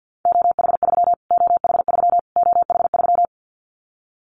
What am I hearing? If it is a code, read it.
Morse code, heard as a keyed tone, about 48 wpm, 700 Hz. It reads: OH3 OH3 OH3
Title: Callsign Prefixes for Top 25 DXCC - Repeated 3x - 50wpm